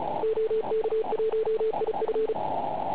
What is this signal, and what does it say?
Morse code, heard as a keyed tone, about 33 wpm, 440 Hz, with radio noise. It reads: OK1IF
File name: Pile UP CUC204